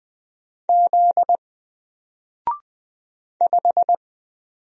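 Morse code, heard as a keyed tone, about 20 wpm, 700 Hz. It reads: Z 5
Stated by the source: Single letter and number Morse drill at 20 wpm, from N4LCD.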